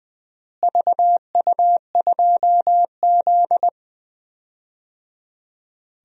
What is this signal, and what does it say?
Morse code, heard as a keyed tone, about 21 wpm, 700 Hz. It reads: VU2Z
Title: DXCC-100 Call Signs - 4 Characters - Repeated 3x - 20wpm